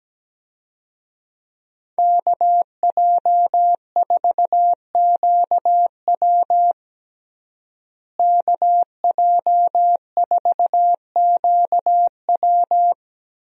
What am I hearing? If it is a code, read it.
Morse code, heard as a keyed tone, about 17 wpm, 700 Hz. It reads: KJ4QW KJ4QW
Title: USA Call Signs - 5 Characters - Repeated 3x - 17wpm